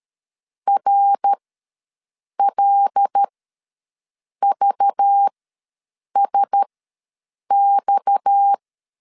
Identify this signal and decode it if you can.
Morse code, heard as a keyed tone, about 13 wpm, spaced as for 6 wpm, 800 Hz. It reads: RLVSX
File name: LEZIONI / 6 PKX0 / 6 PKX ws 12wpm 3106